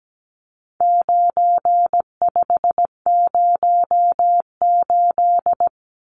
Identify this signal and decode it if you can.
Morse code, heard as a keyed tone, about 17 wpm, 700 Hz. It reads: 9508
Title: Numbers - 4-Digit 17wpm